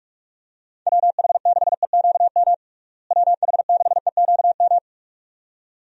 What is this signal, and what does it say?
Morse code, heard as a keyed tone, about 42 wpm, 700 Hz. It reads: WH6EQM WH6EQM
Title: USA Call Signs - 6 Characters - Repeated 2x - 45wpm